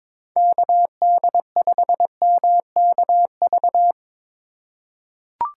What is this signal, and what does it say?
Morse code, heard as a keyed tone, about 22 wpm, 700 Hz. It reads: KD5MKV
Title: USA Call Signs - 6 Characters - 4x Character Spacing - 22wpm